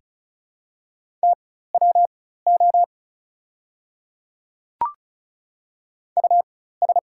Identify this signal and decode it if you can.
Morse code, heard as a keyed tone, about 36 wpm, spaced as for 16 wpm, 690 Hz. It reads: TWO US
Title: 4x Character Spacing - Top 200 Words - 35wpm